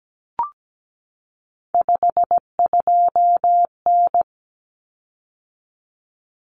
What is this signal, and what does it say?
Morse code, heard as a keyed tone, about 17 wpm, 700 Hz. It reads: 52N
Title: Common Cut Numbers - 3-Digit 17wpm